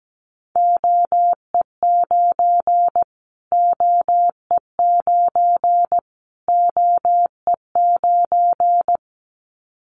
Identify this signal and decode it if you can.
Morse code, heard as a keyed tone, about 17 wpm, 700 Hz. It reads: OE9 OE9 OE9
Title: Callsign Prefixes for Top 25 DXCC - Repeated 3x - 17wpm